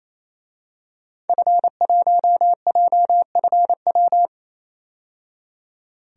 Morse code, heard as a keyed tone, about 27 wpm, 700 Hz. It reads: F1JFW
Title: DXCC-100 Call Signs - 5 Characters - 28wpm